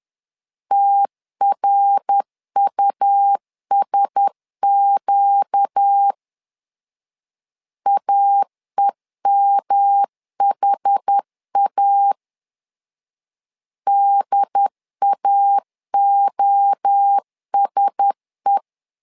Morse code, heard as a keyed tone, about 10 wpm, 800 Hz. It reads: TRUSQ AEMHA DAOSE
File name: LEZIONI / 4 Q1S 10wpm 2133